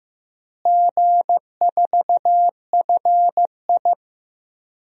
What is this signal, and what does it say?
Morse code, heard as a keyed tone, about 15 wpm, 700 Hz. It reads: G4FI
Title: DXCC-100 Call Signs - 4 Characters - Repeated 3x - 15wpm